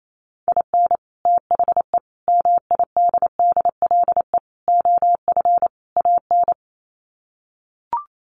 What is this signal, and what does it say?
Morse code, heard as a keyed tone, about 28 wpm, 700 Hz. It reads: IN THE MIDDLE OF AN
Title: Sets of 5 Words Encore2 28wpm